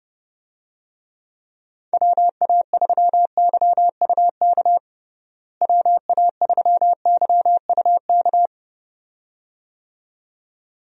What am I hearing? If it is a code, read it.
Morse code, heard as a keyed tone, about 30 wpm, 700 Hz. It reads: WA3YUK WA3YUK
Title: USA Call Signs - 6 Characters - Repeated 2x - 30wpm